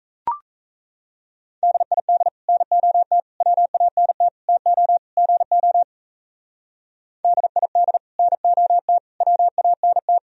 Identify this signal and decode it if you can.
Morse code, heard as a keyed tone, about 33 wpm, 700 Hz. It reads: DID NOT WANT TO GO DID NOT WANT
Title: Speed-Racing - Sets of 5 Words 28wpm